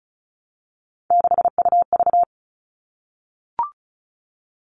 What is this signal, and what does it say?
Morse code, heard as a keyed tone, about 34 wpm, 700 Hz. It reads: 6UV
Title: All Cut Numbers - 3-Digit 35wpm